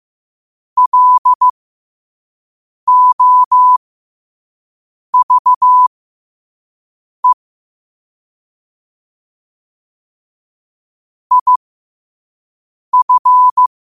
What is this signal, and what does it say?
Morse code, heard as a keyed tone, about 15 wpm, spaced as for 5 wpm, 1000 Hz. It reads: LOVE IF
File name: qotdfiles / 5wpm